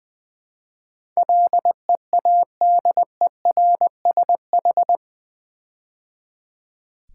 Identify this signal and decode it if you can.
Morse code, heard as a keyed tone, about 20 wpm, 700 Hz. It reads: LEADERSH
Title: Build A Word - 4 Letters At A Time - 10-Letter Words - 20wpm